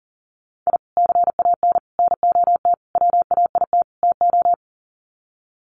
Morse code, heard as a keyed tone, about 40 wpm, 700 Hz. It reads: I CAN NOT WAIT TO
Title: Sets of 5 Words Encore 40wpm